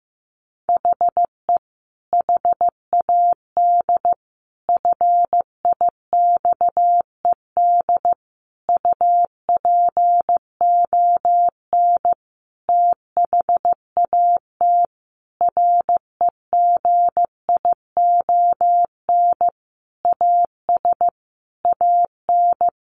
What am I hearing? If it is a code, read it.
Morse code, heard as a keyed tone, about 15 wpm, 700 Hz. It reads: HE HAD FIXED UPON THAT REGION AS AN
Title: Sentences from Top 1500 Words 15wpm